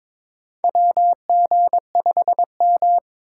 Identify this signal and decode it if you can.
Morse code, heard as a keyed tone, about 22 wpm, 700 Hz. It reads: WG5M
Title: USA Call Signs - 4 Characters - 6x Character Spacing - 22wpm